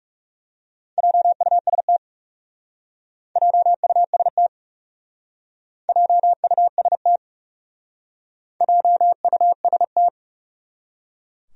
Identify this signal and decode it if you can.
Morse code, heard as a keyed tone, about 39 wpm, 700 Hz. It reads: JUST JUST JUST JUST